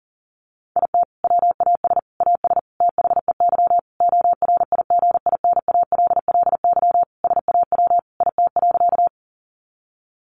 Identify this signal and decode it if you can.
Morse code, heard as a keyed tone, about 40 wpm, 700 Hz. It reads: IT WAS AS THEY ORIGINALLY SAW IT.
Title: Sentences from Top 1500 Words 40wpm